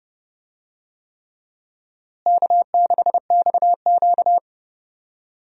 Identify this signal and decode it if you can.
Morse code, heard as a keyed tone, about 30 wpm, 700 Hz. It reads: K6XQ
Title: USA Call Signs - 4 Characters - Repeated 3x - 30wpm